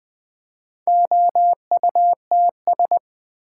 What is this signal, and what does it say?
Morse code, heard as a keyed tone, about 20 wpm, 700 Hz. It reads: OUTS
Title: Build A Word - 4 Letters At A Time - 11-Letter Words - 20wpm